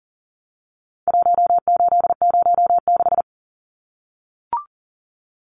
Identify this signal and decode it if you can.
Morse code, heard as a keyed tone, about 40 wpm, 700 Hz. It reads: 1806
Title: Numbers - 4-Digit 40wpm